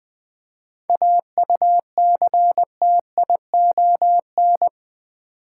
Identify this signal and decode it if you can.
Morse code, heard as a keyed tone, about 20 wpm, 700 Hz. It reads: AUCTION
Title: 7 Letter Words - First 3 Letters Given - 20wpm